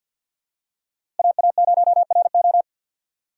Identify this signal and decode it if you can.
Morse code, heard as a keyed tone, about 51 wpm, 700 Hz. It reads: AA0RO
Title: USA Call Signs - 5 Characters - Repeated 3x - 50wpm